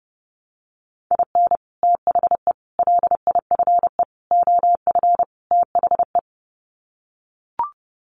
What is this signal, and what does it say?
Morse code, heard as a keyed tone, about 30 wpm, 700 Hz. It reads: IN THE LIFE OF THE
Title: Sets of 5 Words Encore2 30wpm